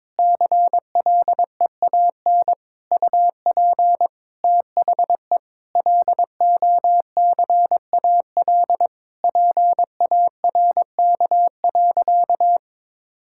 Morse code, heard as a keyed tone, about 22 wpm, 700 Hz. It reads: CLEAN UP THE LOCAL PARK.
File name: Difficult Sentences with Hints - 22wpm